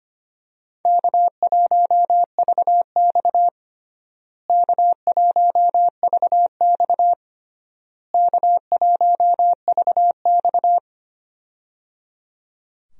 Morse code, heard as a keyed tone, about 25 wpm, 700 Hz. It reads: K1VX K1VX K1VX
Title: USA Call Signs - 4 Characters - Repeated 3x - 25wpm